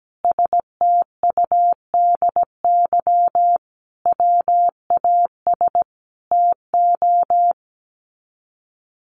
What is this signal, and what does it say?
Morse code, heard as a keyed tone, about 17 wpm, 700 Hz. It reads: STUDY WAS TO
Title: Sets of 5 Words 17wpm